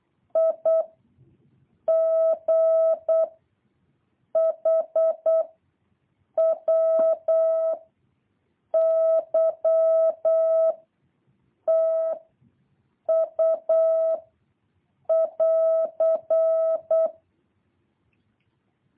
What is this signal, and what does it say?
Morse code, heard as a keyed tone, about 8 wpm, 650 Hz. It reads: I G H W Y T U <AR>
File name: jh8cbh / 2023cw